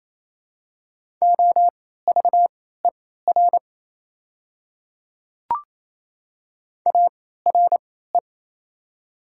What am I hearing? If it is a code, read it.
Morse code, heard as a keyed tone, about 28 wpm, spaced as for 16 wpm, 700 Hz. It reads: OVER ARE
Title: 3x Character Spacing - Top 100 Words - 28wpm